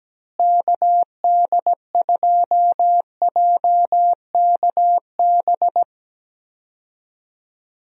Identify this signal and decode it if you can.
Morse code, heard as a keyed tone, about 17 wpm, 700 Hz. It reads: KD2JKB
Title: USA Call Signs - 6 Characters - Repeated 2x - 17wpm